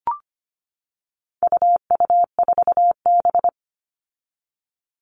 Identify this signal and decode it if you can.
Morse code, heard as a keyed tone, about 25 wpm, 700 Hz. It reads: UU4B
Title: All Cut Numbers - 4-Digit 25wpm